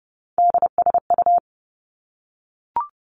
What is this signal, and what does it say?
Morse code, heard as a keyed tone, about 30 wpm, 700 Hz. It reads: DSU
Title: Three Challenging Letter-Number Combinations 30wpm